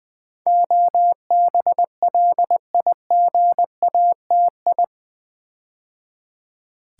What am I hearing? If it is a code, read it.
Morse code, heard as a keyed tone, about 20 wpm, 700 Hz. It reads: OBLIGATI
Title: Build A Word - 4 Letters At A Time - 10-Letter Words - 20wpm